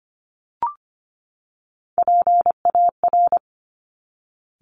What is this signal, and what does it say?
Morse code, heard as a keyed tone, about 25 wpm, 700 Hz. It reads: PAR
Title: Top Three Letter Combinations in Words 25wpm